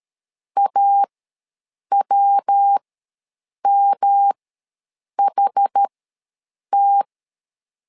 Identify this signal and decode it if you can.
Morse code, heard as a keyed tone, about 13 wpm, spaced as for 7 wpm, 800 Hz. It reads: AWMHT
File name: LEZIONI / 3 MWRO / 3 MW ws 12wpm 1816